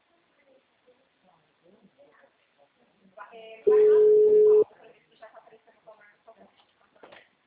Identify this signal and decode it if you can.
Ringback tone (Europe, 425 Hz), with speech.
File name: out- -2014- - - .67